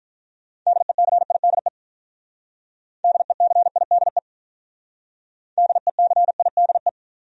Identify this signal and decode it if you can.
Morse code, heard as a keyed tone, about 46 wpm, 700 Hz. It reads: DECIDE DECIDE DECIDE